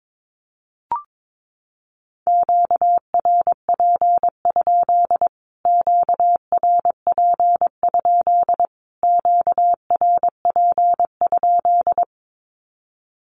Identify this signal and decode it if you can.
Morse code, heard as a keyed tone, about 22 wpm, 700 Hz. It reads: QRP? QRP? QRP?
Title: Q-Codes Repeated 3x 22wpm